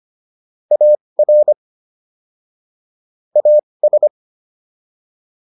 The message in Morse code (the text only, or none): AR AS